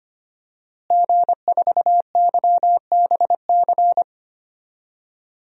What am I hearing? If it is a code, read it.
Morse code, heard as a keyed tone, about 25 wpm, 700 Hz. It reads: G4YBC
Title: DXCC-100 Call Signs - 5 Characters - Repeated 2x - 25wpm